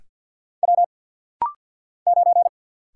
Morse code, heard as a keyed tone, about 50 wpm, 700 Hz. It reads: W 9
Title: Single Letter-Number Alt - Rapid-Fire 50wpm